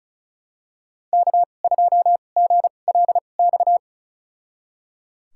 Morse code, heard as a keyed tone, about 36 wpm, spaced as for 25 wpm, 700 Hz. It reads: K2GLX
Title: USA Call Signs - 2x Character Spacing - 35wpm